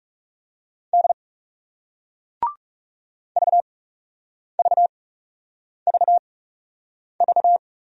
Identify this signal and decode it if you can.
Morse code, heard as a keyed tone, about 39 wpm, spaced as for 8 wpm, 700 Hz. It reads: D VVVV